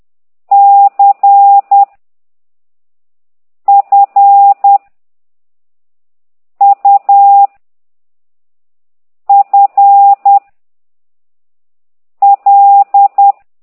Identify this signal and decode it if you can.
Morse code, heard as a keyed tone, about 10 wpm, 800 Hz. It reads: C F U F L
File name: ct2idw / exerc15